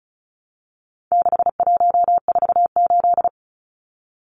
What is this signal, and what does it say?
Morse code, heard as a keyed tone, about 34 wpm, 700 Hz. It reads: 6148
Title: Numbers - 4-Digit 35wpm